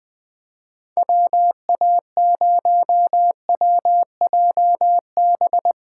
Morse code, heard as a keyed tone, about 20 wpm, 700 Hz. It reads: WA0WJB
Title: USA Callsign DE Callsign - 6 Characters - 20wpm